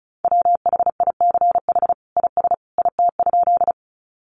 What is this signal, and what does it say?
Morse code, heard as a keyed tone, about 34 wpm, 700 Hz. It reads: WHICH IS IT?
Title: Sentences from Top 100 Words 35wpm